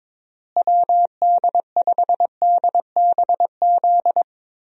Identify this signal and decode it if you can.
Morse code, heard as a keyed tone, about 22 wpm, 700 Hz. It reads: WD5DBZ
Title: USA Call Signs - 6 Characters - Repeated 2x - 22wpm